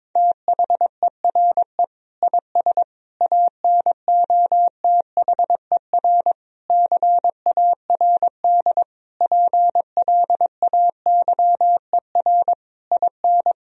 Morse code, heard as a keyed tone, about 22 wpm, 700 Hz. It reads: THERE IS ANOTHER CARD PLAYER IN
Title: Sentences from Top 1000 Words - Review Part 4 - 22wpm